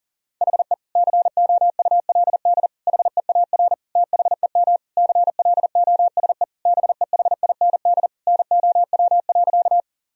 Morse code, heard as a keyed tone, about 40 wpm, 700 Hz. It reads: HE COULD HEAR THEM CLOSE BEHIND NOW.
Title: Sentences from Top 1000 Words - Review Part 1 - 40wpm